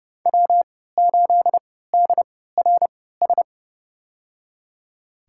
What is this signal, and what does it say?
Morse code, heard as a keyed tone, about 30 wpm, spaced as for 17 wpm, 700 Hz. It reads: W8DRS
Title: USA Call Signs - 5 Characters - 3x Character Spacing - 30wpm